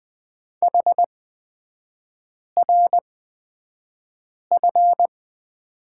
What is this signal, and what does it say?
Morse code, heard as a keyed tone, about 20 wpm, spaced as for 5 wpm, 700 Hz. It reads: HRF
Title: 255 Lesson 41e - Period - Three Characters 20wpm